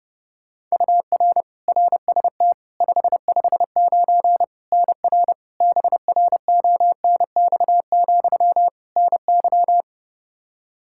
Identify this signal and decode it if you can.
Morse code, heard as a keyed tone, about 30 wpm, 700 Hz. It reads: UR RST 559 NR BRONX, NY
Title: K1USN SST QSO Practice 30wpm